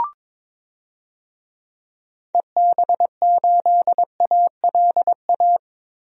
Morse code, heard as a keyed tone, about 22 wpm, 700 Hz. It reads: EB8ALA
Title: DXCC-100 Call Signs - 6 Characters - 22wpm